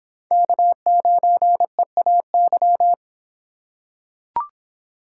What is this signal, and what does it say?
Morse code, heard as a keyed tone, about 24 wpm, 700 Hz. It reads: K9EAY